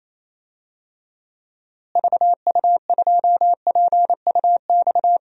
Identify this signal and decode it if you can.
Morse code, heard as a keyed tone, about 28 wpm, 700 Hz. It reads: VU2PUX
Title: DXCC-100 Call Signs - 6 Characters - Repeated 3x - 28wpm